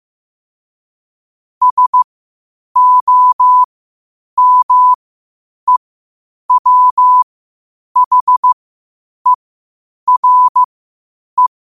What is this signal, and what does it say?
Morse code, heard as a keyed tone, about 15 wpm, spaced as for 8 wpm, 1000 Hz. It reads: SOMEWHERE
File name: qotdfiles / 8wpm